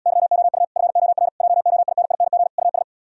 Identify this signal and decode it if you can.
Morse code, heard as a keyed tone, about 77 wpm, 700 Hz. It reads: YOU FOR JOINING US